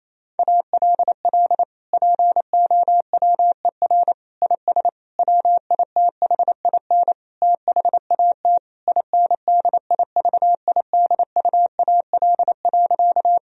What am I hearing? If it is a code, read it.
Morse code, heard as a keyed tone, about 28 wpm, 700 Hz. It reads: ALL POWER IS WITHIN THAT INDIVIDUAL.
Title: Sentences from Top 1000 Words - Review Part 4 - 28wpm